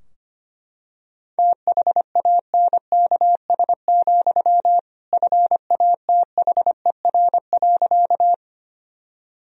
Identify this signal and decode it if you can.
Morse code, heard as a keyed tone, about 25 wpm, 700 Hz. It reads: THANKS, FATHER.
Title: Sentences from Top 1000 Words - Review Part 3 - 25wpm